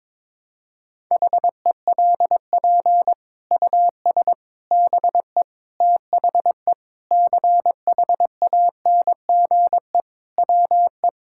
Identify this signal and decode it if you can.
Morse code, heard as a keyed tone, about 21 wpm, 700 Hz. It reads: HELP US BE THE CHANGE WE
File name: practice / 22wpm / Sets of 7 Words - 22wpm